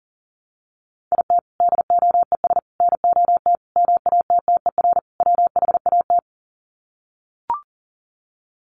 Morse code, heard as a keyed tone, about 40 wpm, 700 Hz. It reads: IT DOES NOT MATTER WHAT
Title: Sets of 5 Words Encore 40wpm